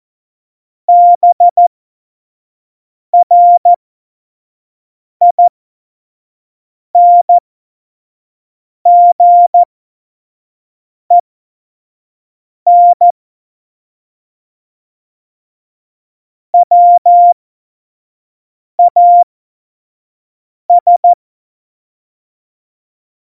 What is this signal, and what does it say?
Morse code, heard as a keyed tone, about 14 wpm, spaced as for 5 wpm, 700 Hz. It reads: BRINGEN WAS
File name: Morsekurs / Witze / Witz 26 Mai 2020, 25 F70